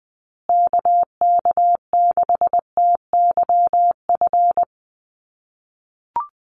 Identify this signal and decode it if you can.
Morse code, heard as a keyed tone, about 20 wpm, 700 Hz. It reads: KK6TYF
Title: USA Call Signs Repeated 2x 20wpm